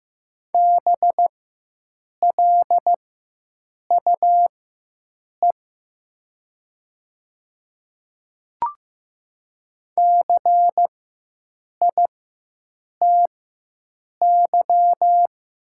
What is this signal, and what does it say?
Morse code, heard as a keyed tone, about 15 wpm, spaced as for 7 wpm, 700 Hz. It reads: BLUE CITY
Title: 4x Character Spacing - Top 500 Words - 15wpm